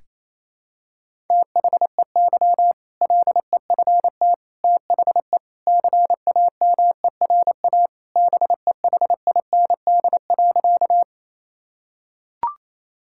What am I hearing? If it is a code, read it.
Morse code, heard as a keyed tone, about 27 wpm, 700 Hz. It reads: THEY LEFT THE CAMERA BEHIND.
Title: Sentences from Top 1000 Words - Review Part 5 - 28wpm